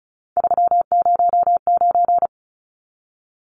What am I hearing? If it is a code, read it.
Morse code, heard as a keyed tone, about 34 wpm, 700 Hz. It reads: 309
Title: Numbers - 3-Digit 35wpm